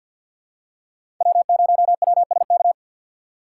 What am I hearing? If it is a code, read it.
Morse code, heard as a keyed tone, about 48 wpm, 700 Hz. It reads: W0WSK
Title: USA Call Signs - 5 Characters - 3x Character Spacing - 50wpm